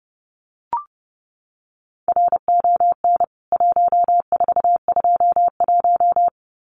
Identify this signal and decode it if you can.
Morse code, heard as a keyed tone, about 30 wpm, 700 Hz. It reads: RON 1421
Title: CWT Exchange 30wpm